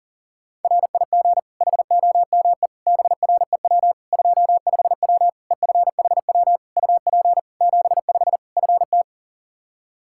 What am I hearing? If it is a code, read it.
Morse code, heard as a keyed tone, about 40 wpm, 700 Hz. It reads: RIG HOME BREW 25W EFHW UP 75 FT
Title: K1USN SST QSO Practice 40wpm